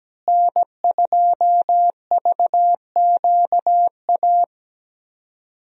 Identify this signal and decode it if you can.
Morse code, heard as a keyed tone, about 17 wpm, 700 Hz. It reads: N2VQA
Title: USA Call Signs - 5 Characters - Repeated 2x - 17wpm